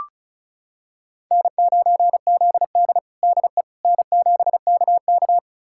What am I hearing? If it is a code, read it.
Morse code, heard as a keyed tone, about 36 wpm, 700 Hz. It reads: N9ZD DE N7KK
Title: USA Callsign DE Callsign - 4 Characters - 35wpm